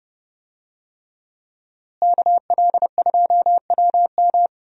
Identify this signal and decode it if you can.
Morse code, heard as a keyed tone, about 32 wpm, 700 Hz. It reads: KL2WM
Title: USA Call Signs - 5 Characters - Repeated 3x - 30wpm